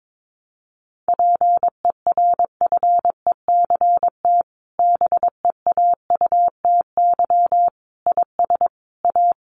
Morse code, heard as a keyed tone, about 22 wpm, 700 Hz. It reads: PERFECT BEAUTY IS A
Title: Sentences from Top 1200 Words 22wpm